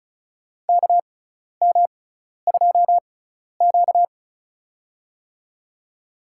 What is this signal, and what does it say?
Morse code, heard as a keyed tone, about 36 wpm, spaced as for 12 wpm, 700 Hz. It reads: KM2Q
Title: USA Call Signs - 6x Character Spacing - 35wpm